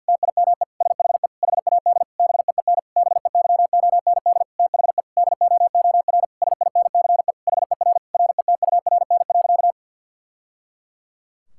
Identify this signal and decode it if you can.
Morse code, heard as a keyed tone, about 51 wpm, 700 Hz. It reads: TIME SHE HAD BEEN BEYOND THE DOOR SINCE HER RETURN.
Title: Sentences from Top 1000 Words - Review Part 6 - 50wpm